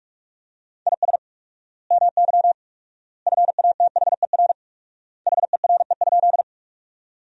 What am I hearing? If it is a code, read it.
Morse code, heard as a keyed tone, about 41 wpm, 700 Hz. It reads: IS MY FATHER HERE?
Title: 4x Word-Spacing - Sentences from Top 300 Words - 45wpm